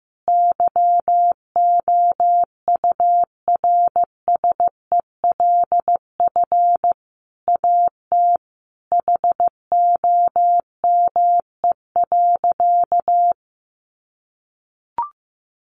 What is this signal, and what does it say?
Morse code, heard as a keyed tone, about 15 wpm, 700 Hz. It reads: YOURSELF AT HOME.
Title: Sentences from Top 600 Words 15wpm